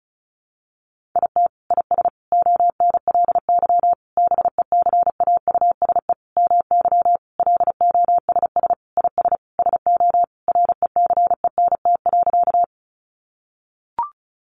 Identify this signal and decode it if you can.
Morse code, heard as a keyed tone, about 34 wpm, 700 Hz. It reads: IT IS ONLY BECAUSE MY LOSS IS SO RECENT.
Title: Sentences from Top 1400 Words 35wpm